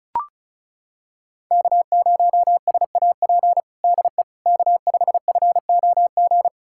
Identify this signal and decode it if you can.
Morse code, heard as a keyed tone, about 36 wpm, 700 Hz. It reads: K0SAP DE K5FOG
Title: USA Callsign DE Callsign - 5 Characters - 35wpm